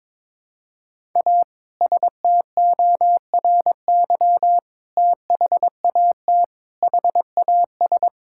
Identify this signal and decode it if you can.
Morse code, heard as a keyed tone, about 22 wpm, 700 Hz. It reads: A STORY THAT HAS
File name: practice / 22wpm / Sentences with 2-Word Phrases - 22wpm